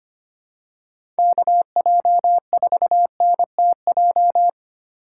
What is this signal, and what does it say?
Morse code, heard as a keyed tone, about 25 wpm, 700 Hz. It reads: KJ4NTJ
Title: USA Call Signs - 6 Characters - 4x Character Spacing - 25wpm